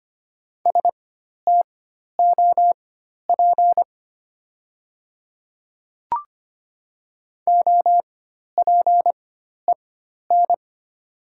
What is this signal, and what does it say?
Morse code, heard as a keyed tone, about 25 wpm, spaced as for 12 wpm, 700 Hz. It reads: STOP OPEN
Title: 4x Character Spacing - Top 300 Words - 25wpm